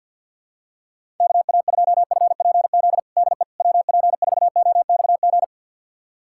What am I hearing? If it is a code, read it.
Morse code, heard as a keyed tone, about 48 wpm, 700 Hz. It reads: KA2FPZ DE WP4OXG
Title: USA Callsign DE Callsign - 6 Characters - 50wpm